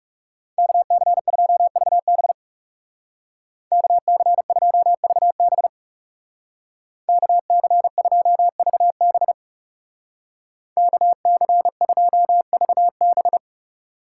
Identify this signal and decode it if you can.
Morse code, heard as a keyed tone, about 36 wpm, 700 Hz. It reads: KC2VB KC2VB KC2VB KC2VB
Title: Speed Racing - USA 2x2 Call Signs 30wpm